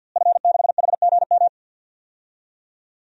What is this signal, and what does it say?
Morse code, heard as a keyed tone, about 46 wpm, 700 Hz. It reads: F6HGM
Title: DXCC-100 Call Signs - 5 Characters - Repeated 2x - 50wpm